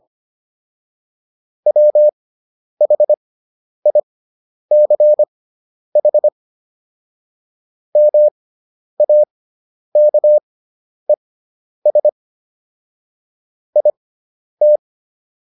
Morse code, heard as a keyed tone, about 25 wpm, spaced as for 10 wpm, 600 Hz. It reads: WHICH MAKES IT